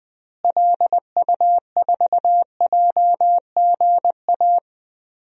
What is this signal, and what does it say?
Morse code, heard as a keyed tone, about 20 wpm, 700 Hz. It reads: LU4JGA
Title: DXCC-100 Call Signs - 6 Characters - Repeated 3x - 20wpm